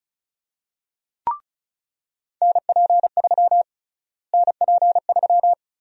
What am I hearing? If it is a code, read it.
Morse code, heard as a keyed tone, about 36 wpm, 700 Hz. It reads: NP3 NP3
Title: USA Call Sign Prefixes - Repeated 2x - 35wpm